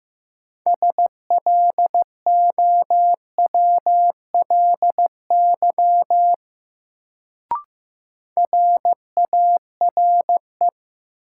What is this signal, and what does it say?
Morse code, heard as a keyed tone, about 15 wpm, 700 Hz. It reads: SLOWLY RARE
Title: Sentences from Top 1100 Words - New Word Drill Alt - 15wpm